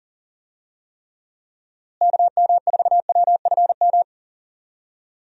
Morse code, heard as a keyed tone, about 40 wpm, 700 Hz. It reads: KM4WFM
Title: USA Call Signs - 6 Characters - Repeated 3x - 40wpm